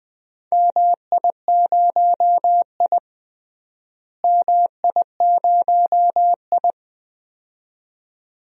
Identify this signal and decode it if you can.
Morse code, heard as a keyed tone, about 20 wpm, 700 Hz. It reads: MI0I MI0I
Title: DXCC-100 Call Signs - 4 Characters - Repeated 2x - 20wpm